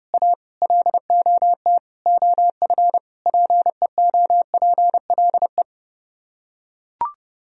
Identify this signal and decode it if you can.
Morse code, heard as a keyed tone, about 30 wpm, 700 Hz. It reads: A LOT OF PEOPLE